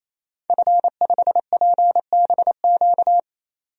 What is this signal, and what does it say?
Morse code, heard as a keyed tone, about 28 wpm, 700 Hz. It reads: F5PBQ
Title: DXCC-100 Call Signs - 5 Characters - 28wpm